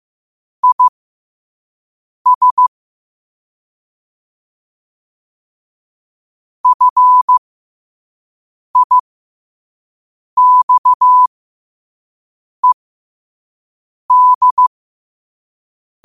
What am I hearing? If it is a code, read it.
Morse code, heard as a keyed tone, about 15 wpm, spaced as for 5 wpm, 1000 Hz. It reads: IS FIXED
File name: qotdfiles / 5wpm